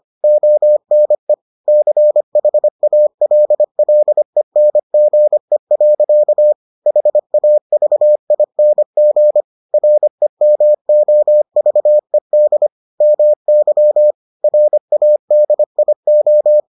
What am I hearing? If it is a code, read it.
Morse code, heard as a keyed tone, about 25 wpm, 600 Hz. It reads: ONE CHALLENGE. HAVING REMOVED MY RADIO